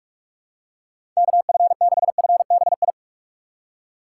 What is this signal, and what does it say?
Morse code, heard as a keyed tone, about 47 wpm, 700 Hz. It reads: KF6FBI
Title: USA Call Signs - 6 Characters - 8x Character Spacing - 45wpm